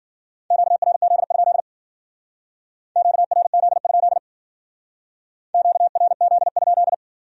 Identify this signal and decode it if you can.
Morse code, heard as a keyed tone, about 53 wpm, 700 Hz. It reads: QRZ? QRZ? QRZ?